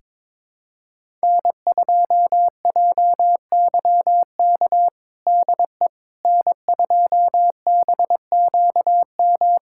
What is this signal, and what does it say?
Morse code, heard as a keyed tone, about 22 wpm, 700 Hz. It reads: N2JYK DE N2BQM
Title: USA Callsign DE Callsign - 5 Characters - 22wpm